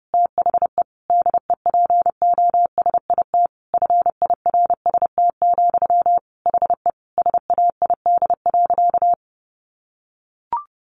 Morse code, heard as a keyed tone, about 28 wpm, 700 Hz. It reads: THE DEPOSIT FIRST, HE SAID.
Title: Sentences from Top 1300 Words 30wpm